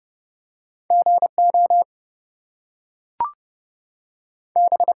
Morse code, heard as a keyed tone, about 30 wpm, 700 Hz. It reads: GO B
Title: Two Letter Words - 30wpm